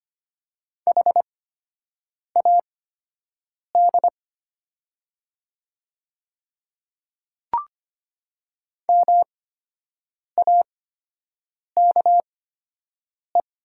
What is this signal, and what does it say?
Morse code, heard as a keyed tone, about 25 wpm, spaced as for 7 wpm, 700 Hz. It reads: HAD MAKE
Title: 8x Character Spacing - Top 100 Words - 25wpm